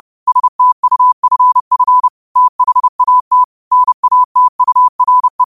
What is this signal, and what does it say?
Morse code, heard as a keyed tone, about 30 wpm, 1000 Hz. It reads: STUFF THAT NATURE
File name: qotdfiles / 30wpm